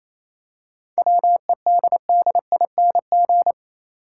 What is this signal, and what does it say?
Morse code, heard as a keyed tone, about 28 wpm, 700 Hz. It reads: WEDDING
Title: Seven Letter Words - 28wpm